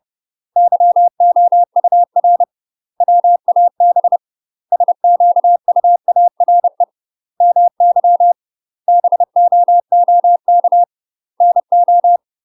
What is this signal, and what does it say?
Morse code, heard as a keyed tone, about 30 wpm, 700 Hz. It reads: YOUR WAB SQUARE MY BOOK NO
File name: UK MASTER QSOs TRAINING / 30WPM / UK MASTER QSOs 0020